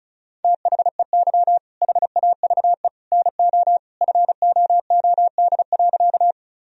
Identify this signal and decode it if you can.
Morse code, heard as a keyed tone, about 34 wpm, 700 Hz. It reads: THEY HAVE NO FOOD.
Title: Sentences from Top 1000 Words - Review Part 1 - 35wpm